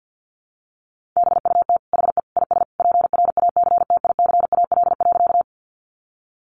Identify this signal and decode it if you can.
Morse code, heard as a keyed tone, about 50 wpm, 700 Hz. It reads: BUT HE IS PRACTICAL.
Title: Sentences from Top 1400 Words 50wpm